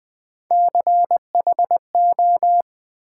Morse code, heard as a keyed tone, about 20 wpm, 700 Hz. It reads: CHO
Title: Build A Word - 1 Letter At A Time - 4-Letter Words - 20wpm